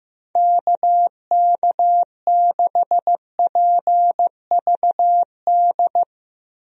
Morse code, heard as a keyed tone, about 15 wpm, 700 Hz. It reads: KK6PVD
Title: USA Call Signs - 6 Characters - 15wpm